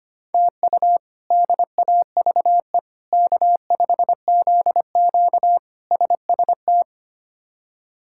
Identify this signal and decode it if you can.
Morse code, heard as a keyed tone, about 25 wpm, 700 Hz. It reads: TU DAVE K5ZQ SST
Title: K1USN SST QSO Practice - 25wpm